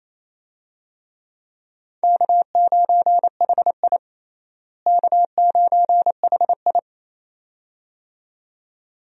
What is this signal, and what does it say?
Morse code, heard as a keyed tone, about 28 wpm, 700 Hz. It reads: K9HI K9HI
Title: USA Call Signs - 4 Characters - Repeated 2x - 28wpm